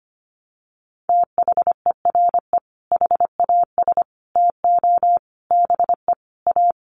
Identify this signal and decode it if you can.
Morse code, heard as a keyed tone, about 25 wpm, 700 Hz. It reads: THERE HAS TO BE A
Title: Sets of 5 Words Encore 25wpm